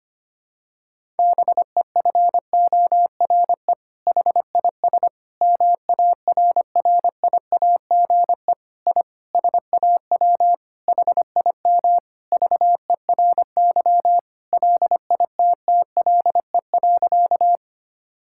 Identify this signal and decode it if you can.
Morse code, heard as a keyed tone, about 25 wpm, 700 Hz. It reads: BEFORE HIS MARRIAGE I SAW HIM VERY LITTLE.
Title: Sentences from Top 1000 Words - Review Part 5 - 25wpm